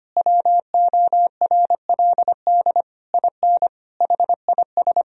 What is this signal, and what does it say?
Morse code, heard as a keyed tone, about 25 wpm, 700 Hz. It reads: WORLD IN HIS
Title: Sentences from Top 1000 Words - Review Part 4 - 25wpm